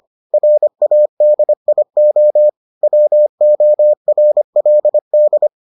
Morse code, heard as a keyed tone, about 25 wpm, 600 Hz. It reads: RADIO WORLD